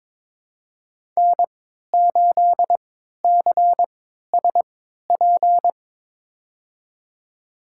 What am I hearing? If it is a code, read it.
Morse code, heard as a keyed tone, about 22 wpm, spaced as for 13 wpm, 700 Hz. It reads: N8CSP
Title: USA Call Signs - 3x Character Spacing - 22wpm